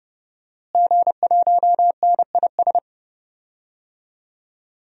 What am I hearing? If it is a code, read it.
Morse code, heard as a keyed tone, about 30 wpm, 700 Hz. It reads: G1NIS
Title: DXCC-100 Call Signs - 5 Characters - Repeated 3x - 30wpm